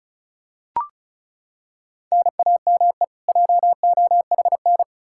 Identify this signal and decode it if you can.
Morse code, heard as a keyed tone, about 36 wpm, 700 Hz. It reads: NAME JOHN
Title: Practice QSO - Exchange 1 - Beginning - 35wpm